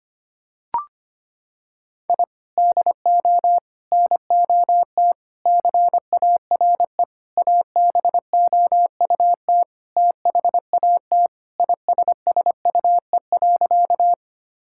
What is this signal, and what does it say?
Morse code, heard as a keyed tone, about 25 wpm, 700 Hz. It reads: I DO NOT CARE ABOUT THAT ISSUE.
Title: Sentences from Top 1000 Words - Review Part 6 - 25wpm